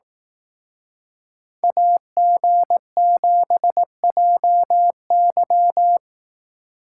Morse code, heard as a keyed tone, about 18 wpm, 700 Hz. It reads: AG7JY
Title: Speed Racing - USA 2x2 Call Signs 15wpm